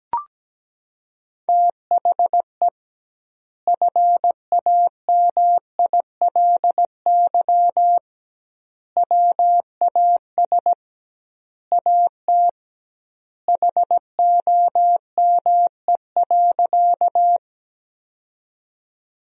Morse code, heard as a keyed tone, about 17 wpm, 700 Hz. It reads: THE FAMILY WAS AT HOME.
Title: 2x Word-Spacing - Sentences from Top 400 Words - 17wpm